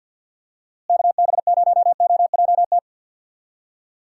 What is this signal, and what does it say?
Morse code, heard as a keyed tone, about 49 wpm, 700 Hz. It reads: KB0OJT
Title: USA Call Signs - 6 Characters - Repeated 2x - 50wpm